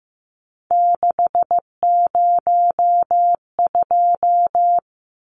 Morse code, heard as a keyed tone, about 15 wpm, 700 Hz. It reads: 602
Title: Numbers - 3-Digit 15wpm